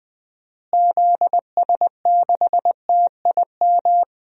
Morse code, heard as a keyed tone, about 20 wpm, 700 Hz. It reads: ZS6TIM
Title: DXCC-100 Call Signs - 6 Characters - Repeated 2x - 20wpm